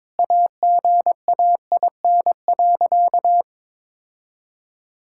Morse code, heard as a keyed tone, about 22 wpm, 700 Hz. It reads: AGAIN.
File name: practice / 22wpm / Sentences with 4-Word Phrases - 22wpm